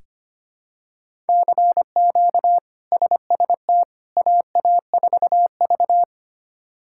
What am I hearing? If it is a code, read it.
Morse code, heard as a keyed tone, about 25 wpm, 700 Hz. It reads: CQ SST AA4V
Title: K1USN SST QSO Practice - 25wpm